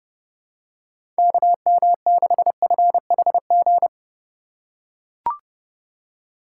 Morse code, heard as a keyed tone, about 29 wpm, 700 Hz. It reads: KM6FHG